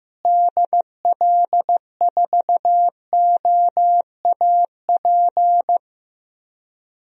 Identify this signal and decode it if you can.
Morse code, heard as a keyed tone, about 15 wpm, 700 Hz. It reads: DL4OAP
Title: USA Call Signs - 6 Characters - Repeated 3x - 15wpm